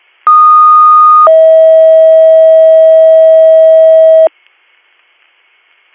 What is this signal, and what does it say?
Two-tone page, heard as a one-second tone, then a three-second tone, about 1230 then 640 Hz. It is heard from 0.3 to 4.3 s.